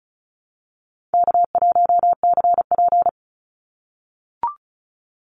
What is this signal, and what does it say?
Morse code, heard as a keyed tone, about 36 wpm, 700 Hz. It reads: K1CP
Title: Call Signs - 35wpm